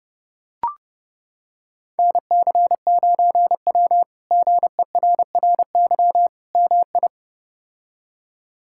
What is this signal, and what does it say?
Morse code, heard as a keyed tone, about 30 wpm, 700 Hz. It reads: NC9W GERRY MI
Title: K1USN SST QSO Practice - 30wpm